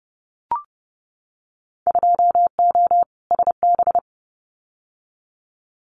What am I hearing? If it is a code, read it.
Morse code, heard as a keyed tone, about 30 wpm, 700 Hz. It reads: 2O SB
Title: Winter Field Day Exchanges 30wpm